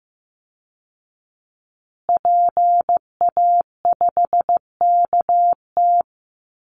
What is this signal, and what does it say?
Morse code, heard as a keyed tone, about 15 wpm, 700 Hz. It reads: PA5KT
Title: Call Signs Repeated 3x - 15wpm